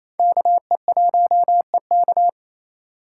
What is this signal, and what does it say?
Morse code, heard as a keyed tone, about 28 wpm, 700 Hz. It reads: KE1EK
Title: USA Call Signs - 5 Characters - 2x Character Spacing - 28wpm